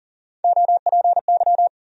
Morse code, heard as a keyed tone, about 40 wpm, 700 Hz. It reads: OPY
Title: USA Call Sign Postfixes - 3 Letters 40wpm